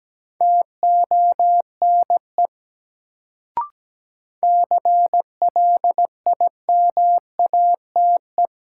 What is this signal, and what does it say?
Morse code, heard as a keyed tone, about 17 wpm, 700 Hz. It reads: TONE CLIMATE
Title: Sentences from Top 1600 Words - New Word Drill Alt - 17wpm